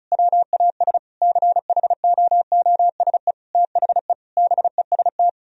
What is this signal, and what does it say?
Morse code, heard as a keyed tone, about 36 wpm, 700 Hz. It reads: WAS CHOOSE THE BEST
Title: Sentences from Top 1000 Words - Review Part 4 - 35wpm